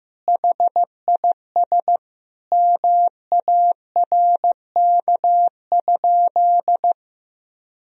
Morse code, heard as a keyed tone, about 15 wpm, 700 Hz. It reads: HIS MARK?
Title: Sentences from Top 300 Words and Abbreviations 15wpm